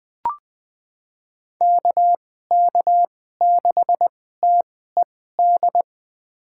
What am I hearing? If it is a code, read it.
Morse code, heard as a keyed tone, about 20 wpm, spaced as for 14 wpm, 700 Hz. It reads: KK6TED
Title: USA Call Signs - 6 Characters - 2x Character Spacing - 20wpm